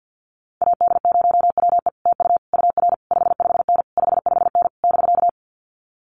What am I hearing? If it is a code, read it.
Morse code, heard as a keyed tone, about 50 wpm, 700 Hz. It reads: AD0WE TU UR 55N 55N <BK>
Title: Common QSO Phrases Encore 50wpm